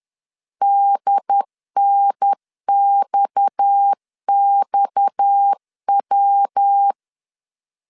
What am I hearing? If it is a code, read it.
Morse code, heard as a keyed tone, about 10 wpm, 800 Hz. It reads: DNXXW
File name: LEZIONI / 6 PKX0 / 6 PKX 10wpm 2426